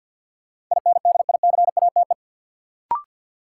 Morse code, heard as a keyed tone, about 51 wpm, 700 Hz. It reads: INDICATE
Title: Sentences from Top 1300 Words - New Word Drill Alt - 50wpm